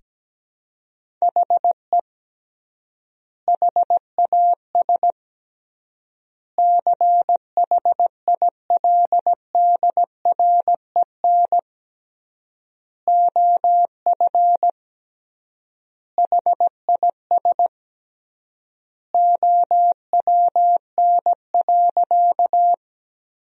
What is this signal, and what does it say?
Morse code, heard as a keyed tone, about 17 wpm, 700 Hz. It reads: HE HAS CHILDREN OF HIS OWN.
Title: 3x Word-Spacing - Sentences from Top 300 Words - 17wpm